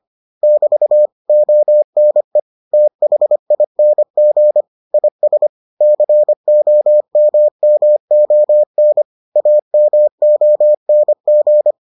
Morse code, heard as a keyed tone, about 25 wpm, 600 Hz. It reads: <BT> ONE THING IS COMMON AMONG